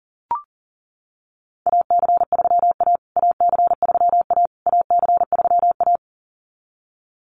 Morse code, heard as a keyed tone, about 40 wpm, 700 Hz. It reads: AC3A AC3A AC3A
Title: USA Call Signs Repeated 3x 40wpm